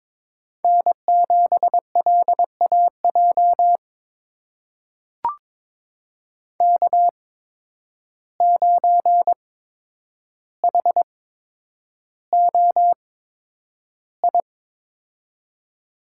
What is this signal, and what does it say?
Morse code, heard as a keyed tone, about 22 wpm, spaced as for 9 wpm, 700 Hz. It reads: N7LAJ K9HOI
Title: USA Call Signs - 5 Characters - 8x Character Spacing - 22wpm